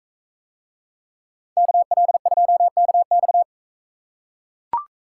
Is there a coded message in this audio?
Yes: Morse code from 1.6 to 3.4 s, heard as a keyed tone, about 44 wpm, 700 Hz. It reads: KL2KX